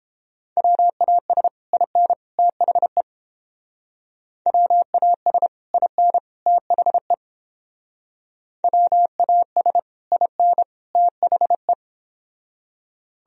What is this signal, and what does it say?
Morse code, heard as a keyed tone, about 29 wpm, 700 Hz. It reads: WAS IN THE WAS IN THE WAS IN THE